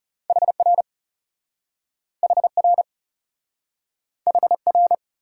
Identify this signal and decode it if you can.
Morse code, heard as a keyed tone, about 40 wpm, 700 Hz. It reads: HR HR HR